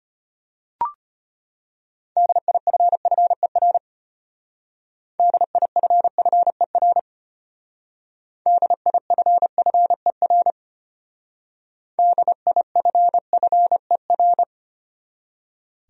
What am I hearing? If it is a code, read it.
Morse code, heard as a keyed tone, about 32 wpm, 700 Hz. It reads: DIFFER DIFFER DIFFER DIFFER